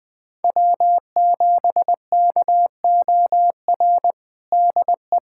Morse code, heard as a keyed tone, about 20 wpm, 700 Hz. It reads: W7KOR DE
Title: USA Callsign DE Callsign 20wpm